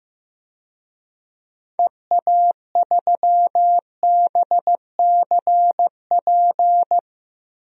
Morse code, heard as a keyed tone, about 15 wpm, 700 Hz. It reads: EA3BCP
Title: DXCC-100 Call Signs - 6 Characters - Repeated 3x - 15wpm